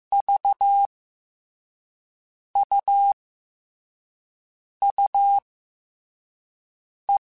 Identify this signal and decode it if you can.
Morse code, heard as a keyed tone, about 15 wpm, 2200 Hz. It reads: V U U E
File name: Abreviations / Abreviations mp3s / Abreviations@13